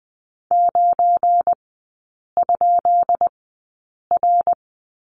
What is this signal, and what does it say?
Morse code, heard as a keyed tone, about 20 wpm, spaced as for 8 wpm, 700 Hz. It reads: 9?R
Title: 218 Lesson 35e - Number 0 - Three Characters 20wpm